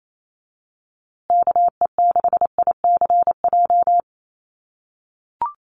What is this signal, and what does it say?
Morse code, heard as a keyed tone, about 28 wpm, 700 Hz. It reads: KE6ICJ
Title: USA Call Signs Repeated 2x 28wpm